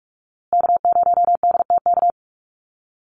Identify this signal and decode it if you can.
Morse code, heard as a keyed tone, about 44 wpm, 700 Hz. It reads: K0DTK
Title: USA Call Signs Repeated 3x 45wpm